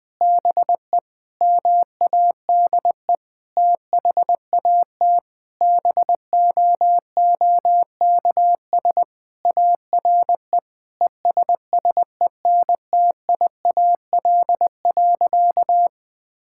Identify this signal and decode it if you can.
Morse code, heard as a keyed tone, about 21 wpm, 700 Hz. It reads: BE MADE THAT BOOKS ARE ESSENTIAL.
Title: Sentences with 8-Word Phrases - 20wpm